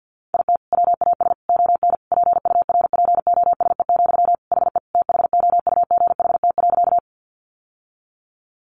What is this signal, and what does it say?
Morse code, heard as a keyed tone, about 50 wpm, 700 Hz. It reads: IT WAS ON PURPOSE, HE THOUGHT.
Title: Sentences from Top 800 Words 50wpm